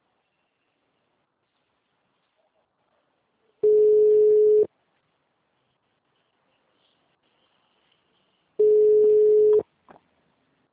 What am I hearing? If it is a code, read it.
Ringback tone (Europe, 425 Hz).